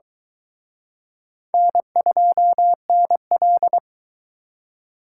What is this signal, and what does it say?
Morse code, heard as a keyed tone, about 23 wpm, 700 Hz. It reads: N2NL